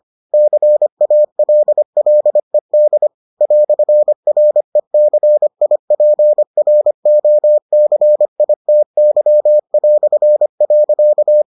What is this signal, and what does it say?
Morse code, heard as a keyed tone, about 25 wpm, 600 Hz. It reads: CALLED "RECIPROCITY".